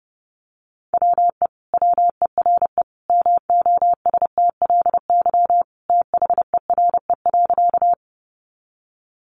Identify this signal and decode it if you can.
Morse code, heard as a keyed tone, about 28 wpm, 700 Hz. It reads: WE WERE MOSTLY THERE.
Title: Sentences from Top 1300 Words 30wpm